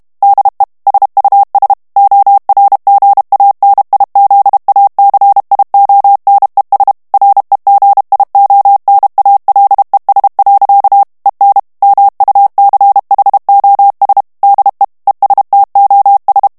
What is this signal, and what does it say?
Morse code, heard as a keyed tone, about 32 wpm, 800 Hz. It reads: DE SUS ORGANIZACIONES REGIONALES. EN MUCHOS DE ESTOS
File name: morse mp3 / ESPAGNOL TEXTE EN CLAIR 1 VITESSE 30 MOTS MINUTES